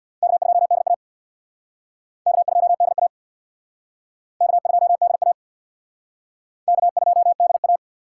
Morse code, heard as a keyed tone, about 62 wpm, 700 Hz. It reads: K2DA K2DA K2DA K2DA
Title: Speed Racing - USA 1x2 Call Signs 50wpm